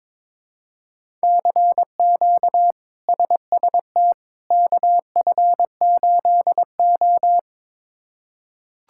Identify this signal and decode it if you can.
Morse code, heard as a keyed tone, about 22 wpm, 700 Hz. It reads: CQ SST KF8O
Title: K1USN SST QSO Practice - 22wpm